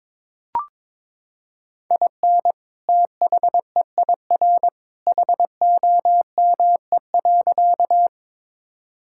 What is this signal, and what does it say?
Morse code, heard as a keyed tone, about 22 wpm, 700 Hz. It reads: IN THEIR HOME.
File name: practice / 22wpm / Sentences with 2-Word Phrases - 22wpm